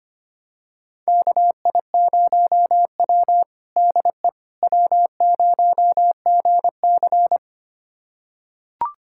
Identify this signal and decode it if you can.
Morse code, heard as a keyed tone, about 25 wpm, 700 Hz. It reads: KI0W DE W0GC
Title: USA Callsign DE Callsign - 4 Characters - 25wpm